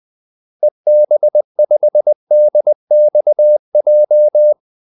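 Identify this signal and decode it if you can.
Morse code, heard as a keyed tone, about 20 wpm, 600 Hz. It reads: EB5DXJ
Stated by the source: Morse call-sign practice at 20 wpm.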